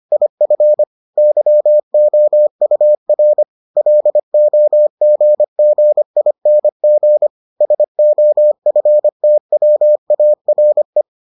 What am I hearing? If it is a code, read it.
Morse code, heard as a keyed tone, about 25 wpm, 600 Hz. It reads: IF YOUR LOGGING SOFTWARE